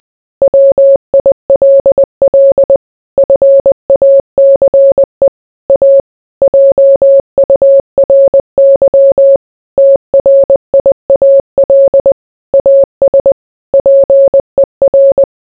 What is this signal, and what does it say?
Morse code, heard as a keyed tone, about 20 wpm, 550 Hz. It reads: WILL FACE A JURY TRIAL AS PER